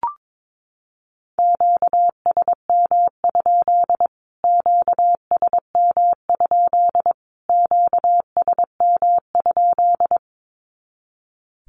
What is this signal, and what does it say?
Morse code, heard as a keyed tone, about 22 wpm, 700 Hz. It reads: QSM? QSM? QSM?
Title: Q-Codes Repeated 3x 22wpm